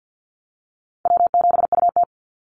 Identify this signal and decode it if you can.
Morse code, heard as a keyed tone, about 50 wpm, 700 Hz. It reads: W7UT
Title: Easy Call Signs - 50wpm